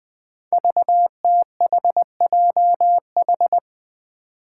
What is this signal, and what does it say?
Morse code, heard as a keyed tone, about 20 wpm, 700 Hz. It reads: VTHJH
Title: Five Challenging Letter-Number Combinations 20wpm